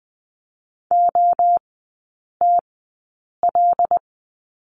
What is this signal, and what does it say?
Morse code, heard as a keyed tone, about 20 wpm, spaced as for 8 wpm, 700 Hz. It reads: OTL
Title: 133 Lesson 22e - Question Mark - Three Characters 20wpm